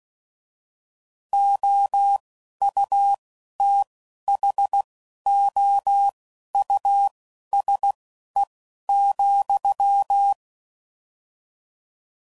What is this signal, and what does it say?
Morse code, heard as a keyed tone, about 16 wpm, spaced as for 11 wpm, 800 Hz. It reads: OUTHOUSE,
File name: Suffixes / General / Houses